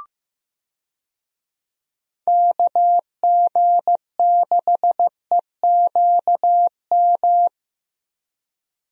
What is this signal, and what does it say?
Morse code, heard as a keyed tone, about 15 wpm, 700 Hz. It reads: KG6EQM
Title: USA Call Signs - 6 Characters - Repeated 3x - 15wpm